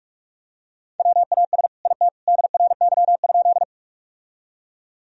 Morse code, heard as a keyed tone, about 46 wpm, 700 Hz. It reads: WAS IT DRY?
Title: Sentences from Top 1000 Words - Review Part 3 - 45wpm